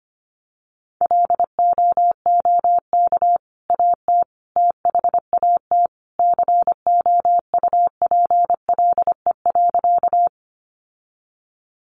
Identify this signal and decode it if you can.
Morse code, heard as a keyed tone, about 25 wpm, 700 Hz. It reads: LOOK AT THAT COUPLE.
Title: Sentences from Top 1100 Words 25wpm